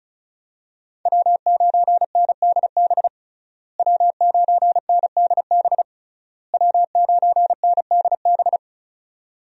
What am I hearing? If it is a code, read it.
Morse code, heard as a keyed tone, about 36 wpm, 700 Hz. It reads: W9NDB W9NDB W9NDB
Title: USA Call Signs - 5 Characters - Repeated 3x - 35wpm